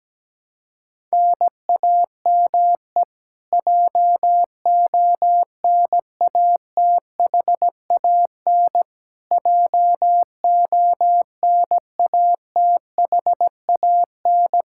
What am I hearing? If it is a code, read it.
Morse code, heard as a keyed tone, about 17 wpm, 700 Hz. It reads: NAME JONATHAN JONATHAN
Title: Practice QSO - Exchange 1 - Beginning - 17wpm